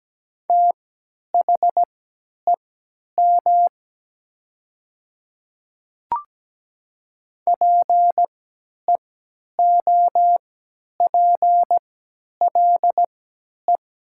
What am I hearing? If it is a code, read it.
Morse code, heard as a keyed tone, about 17 wpm, spaced as for 10 wpm, 700 Hz. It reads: THEM PEOPLE
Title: 3x Character Spacing - Top 100 Words - 17wpm